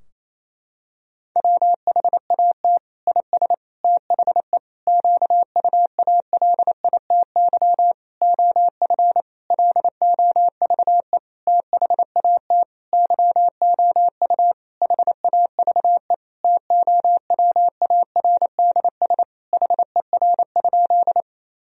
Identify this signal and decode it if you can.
Morse code, heard as a keyed tone, about 28 wpm, 700 Hz. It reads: WHAT IS THE QUALITY OF LOVE THAT YOU HAVE TOWARDS HER?
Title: Sentences from Top 1000 Words - Review Part 4 - 28wpm